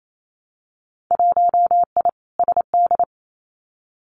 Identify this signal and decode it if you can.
Morse code, heard as a keyed tone, about 28 wpm, 700 Hz. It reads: 1I SD
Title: Winter Field Day Exchanges 28wpm